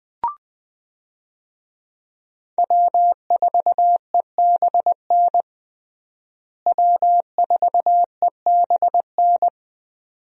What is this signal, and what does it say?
Morse code, heard as a keyed tone, about 21 wpm, 700 Hz. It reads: W4EBN W4EBN
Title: USA Call Signs - 5 Characters - Repeated 2x - 20wpm